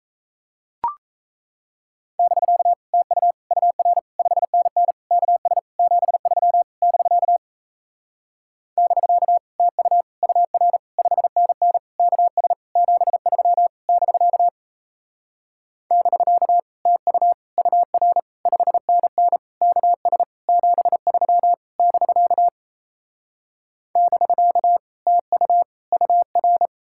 Morse code, heard as a keyed tone, about 37 wpm, 700 Hz. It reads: <BK> TU UR 5NN KS 73 <BK> <BK> TU UR 5NN KS 73 <BK> <BK> TU UR 5NN KS 73 <BK> <BK> TU UR